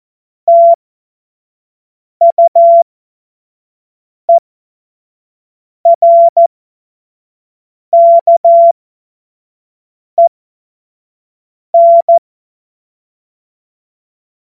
Morse code, heard as a keyed tone, about 14 wpm, spaced as for 5 wpm, 680 Hz. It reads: TUERKEN
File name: Witz-14-Oktober-2025-25-F70